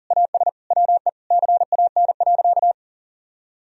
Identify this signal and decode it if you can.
Morse code, heard as a keyed tone, about 40 wpm, 700 Hz. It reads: AS WE CAN.
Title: Sentences from Top 1000 Words - Review Part 2 - 40wpm